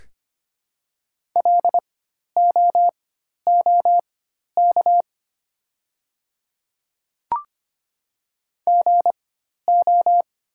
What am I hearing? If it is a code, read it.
Morse code, heard as a keyed tone, about 25 wpm, spaced as for 12 wpm, 700 Hz. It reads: LOOK GO
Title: 4x Character Spacing - Top 200 Words - 25wpm